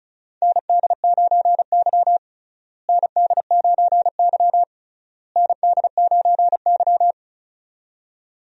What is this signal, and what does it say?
Morse code, heard as a keyed tone, about 36 wpm, 700 Hz. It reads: ND9Y ND9Y ND9Y
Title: USA Call Signs - 4 Characters - Repeated 3x - 35wpm